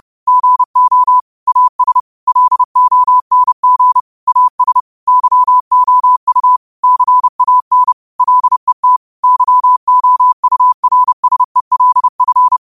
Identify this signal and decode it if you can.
Morse code, heard as a keyed tone, about 30 wpm, 1000 Hz. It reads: GO AS LONG AS YOU CAN LET YOURSELF
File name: qotdfiles / 30wpm